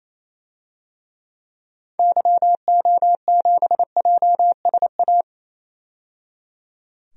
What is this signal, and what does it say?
Morse code, heard as a keyed tone, about 28 wpm, 700 Hz. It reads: YO7JSA
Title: DXCC-100 Call Signs - 6 Characters - 28wpm